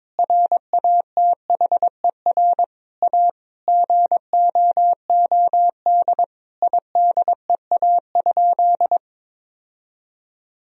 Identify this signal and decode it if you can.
Morse code, heard as a keyed tone, about 22 wpm, 700 Hz. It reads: RATHER A GOOD IDEA?
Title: Sentences from Top 1000 Words - Review Part 4 - 22wpm